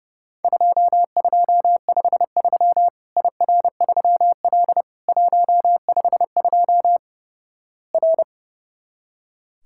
Morse code, heard as a keyed tone, about 30 wpm, 700 Hz. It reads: 2253 IR3L 152 R
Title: Worked All Europe DX Contest - QTC Exchange - 30wpm